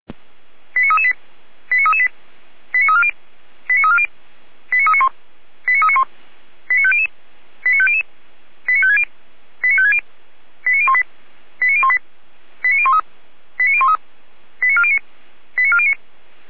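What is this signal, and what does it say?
Five-tone selective call (ZVEI), heard as quick runs of five tones, tones of about 70 ms. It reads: 89338 89338 89340 89340 89391 89391 89500 89500 89560 89560 89018 89018 89013 89013 89409 89409
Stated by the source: Five-tone ZVEI selective call received by SDR.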